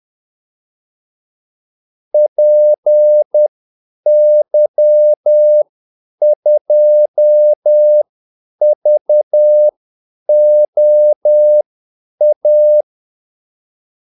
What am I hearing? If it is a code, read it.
Morse code, heard as a keyed tone, about 10 wpm, spaced as for 8 wpm, 600 Hz. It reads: PY2VOA
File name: Callsigns-10-8wpm